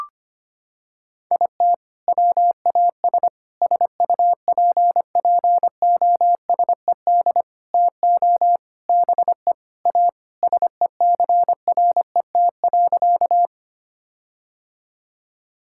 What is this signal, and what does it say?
Morse code, heard as a keyed tone, about 25 wpm, 700 Hz. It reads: IT WAS SUPPOSED TO BE A SECRET.
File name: practice / 25wpm / Sentences with 4-Word Phrases - 25wpm